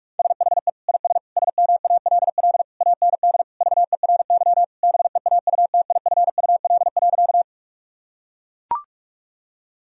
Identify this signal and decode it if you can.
Morse code, heard as a keyed tone, about 41 wpm, 700 Hz. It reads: SHE IS SMALL AND VERY BEAUTIFUL.
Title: Sentences from Top 1000 Words - Review Part 1 - 45wpm